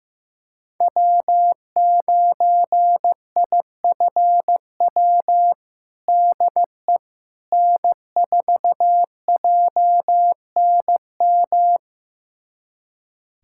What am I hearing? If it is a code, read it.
Morse code, heard as a keyed tone, about 16 wpm, 700 Hz. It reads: W9IFW DE N4JNM
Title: USA Callsign DE Callsign - 5 Characters - 15wpm